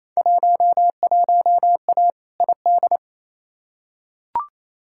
Morse code, heard as a keyed tone, about 28 wpm, 700 Hz. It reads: 11A ID
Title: Field Day Exchanges Encore - 28wpm